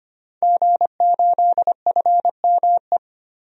Morse code, heard as a keyed tone, about 25 wpm, 700 Hz. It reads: G8FME
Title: DXCC-100 Call Signs - 5 Characters - 25wpm